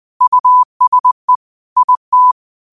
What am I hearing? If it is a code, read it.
Morse code, heard as a keyed tone, about 20 wpm, 1000 Hz. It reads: USE IT